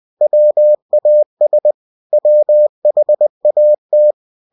Morse code, heard as a keyed tone, about 20 wpm, 600 Hz. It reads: WAS WHAT
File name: cw-books / winnie-the-pooh / winnie-20wpm-page-0063